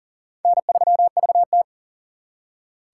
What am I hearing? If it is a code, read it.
Morse code, heard as a keyed tone, about 50 wpm, 700 Hz. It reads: N3VT